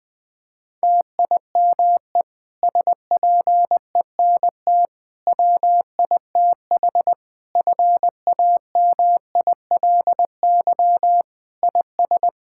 Morse code, heard as a keyed tone, about 21 wpm, 700 Hz. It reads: TIME SPENT WITH FAMILY IS
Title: Difficult Sentences with Hints - 20wpm